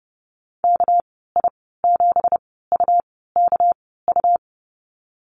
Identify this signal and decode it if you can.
Morse code, heard as a keyed tone, about 30 wpm, spaced as for 17 wpm, 700 Hz. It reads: KI7UKU
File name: Lesson 37g - Letter x - Call Signs 30wpm